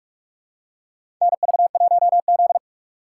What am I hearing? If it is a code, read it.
Morse code, heard as a keyed tone, about 41 wpm, 700 Hz. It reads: NV1Z
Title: USA Call Signs - 3x Character Spacing - 45wpm